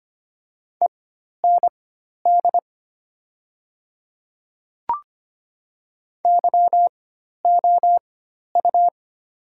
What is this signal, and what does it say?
Morse code, heard as a keyed tone, about 25 wpm, spaced as for 12 wpm, 700 Hz. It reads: END YOU
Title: 4x Character Spacing - Top 300 Words - 25wpm